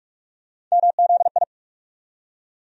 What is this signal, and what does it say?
Morse code, heard as a keyed tone, about 46 wpm, 700 Hz. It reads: MZI
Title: USA Call Sign Postfixes - 3 Letters 45wpm